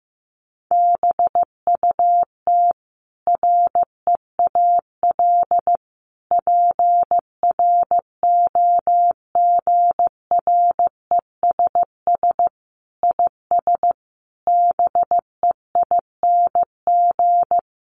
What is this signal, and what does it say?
Morse code, heard as a keyed tone, about 15 wpm, 700 Hz. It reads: BUT REAL PROGRESS IS BEING
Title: Sentences from Top 1400 Words 15wpm